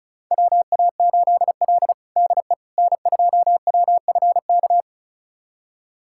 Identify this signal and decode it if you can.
Morse code, heard as a keyed tone, about 34 wpm, 700 Hz. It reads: WA8L DE N2WFK
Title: USA Callsign DE Callsign 35wpm